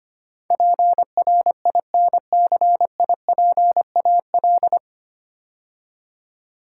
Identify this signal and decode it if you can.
Morse code, heard as a keyed tone, about 25 wpm, 700 Hz. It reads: PRINCIPAL
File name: practice / 25wpm / 9 Letter Words - First 3 Letters Given - 25wpm